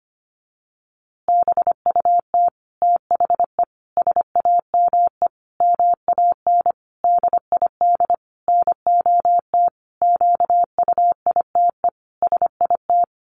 Morse code, heard as a keyed tone, about 25 wpm, 700 Hz. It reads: BUT THE SAME MAN DID NOT QUITE SIT
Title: A Sound Of Thunder 25wpm